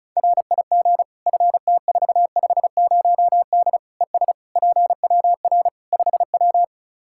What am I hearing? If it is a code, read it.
Morse code, heard as a keyed tone, about 34 wpm, 700 Hz. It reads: RIG FT450D ES PWR 5W
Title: Practice QSO - Exchange 2 - Info - 35wpm